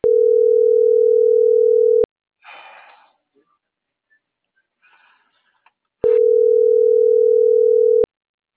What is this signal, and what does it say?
Ringback tone (North America), 440 + 480 Hz.